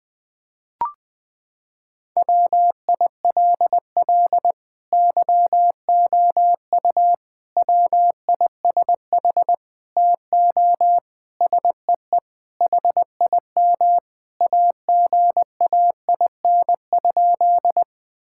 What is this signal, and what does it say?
Morse code, heard as a keyed tone, about 20 wpm, 700 Hz. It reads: WILL YOU WISH TO SEE HIM AGAIN?
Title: Sentences from Top 1600 Words 20wpm